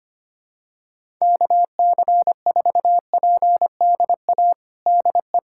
Morse code, heard as a keyed tone, about 25 wpm, 700 Hz. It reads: KC4PDA DE
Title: USA Callsign DE Callsign - 6 Characters - 25wpm